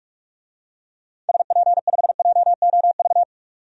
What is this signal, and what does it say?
Morse code, heard as a keyed tone, about 41 wpm, 700 Hz. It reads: SP5JOV
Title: DXCC-100 Call Signs - 6 Characters - 45wpm